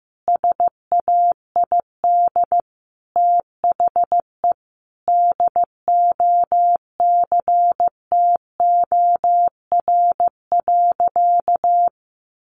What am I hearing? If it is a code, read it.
Morse code, heard as a keyed tone, about 14 wpm, 700 Hz. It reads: SAID THE DOCTOR.
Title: Sentences from Top 1500 Words 15wpm